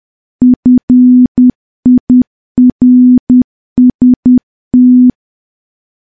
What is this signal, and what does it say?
Morse code, heard as a keyed tone, about 10 wpm, 250 Hz. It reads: FIRST
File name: morse / morse-training-10WPM